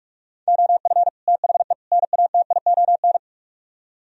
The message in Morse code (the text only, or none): OF THE NATION